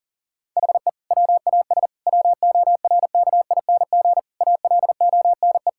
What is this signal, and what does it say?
Morse code, heard as a keyed tone, about 40 wpm, 700 Hz. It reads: HE WAS WORKING ALONE
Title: Sentences from Top 1000 Words - Review Part 5 - 40wpm